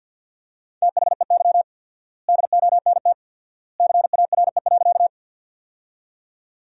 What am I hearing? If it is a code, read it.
Morse code, heard as a keyed tone, about 51 wpm, 700 Hz. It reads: THEY DONT CARE.
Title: 4x Word-Spacing - Sentences from Top 300 Words - 50wpm